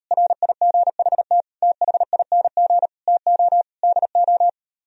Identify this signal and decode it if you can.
Morse code, heard as a keyed tone, about 38 wpm, 700 Hz. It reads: RIGHT THING TO DO